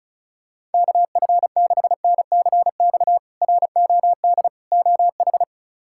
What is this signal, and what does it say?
Morse code, heard as a keyed tone, about 34 wpm, 700 Hz. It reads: KF6NCX ROD OH
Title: K1USN SST QSO Practice - 35wpm